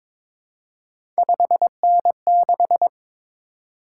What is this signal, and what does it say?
Morse code, heard as a keyed tone, about 22 wpm, 700 Hz. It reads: 5N6
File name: practice / 22wpm / Three Letter-Number Combinations - 22wpm